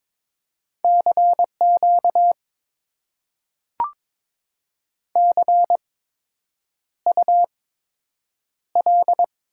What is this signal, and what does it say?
Morse code, heard as a keyed tone, about 22 wpm, spaced as for 8 wpm, 700 Hz. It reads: CQ CUL
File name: Top 90 QSO Elements - 8x Character Spacing - 22wpm